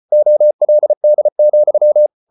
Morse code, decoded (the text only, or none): OLD,